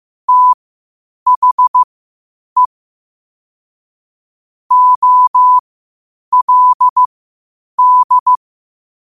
Morse code, heard as a keyed tone, about 15 wpm, spaced as for 8 wpm, 1000 Hz. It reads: THE OLD
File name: qotdfiles / 8wpm